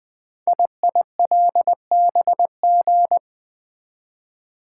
Morse code, heard as a keyed tone, about 20 wpm, 700 Hz. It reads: IILBG
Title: Code Groups - 5 Letters - 20wpm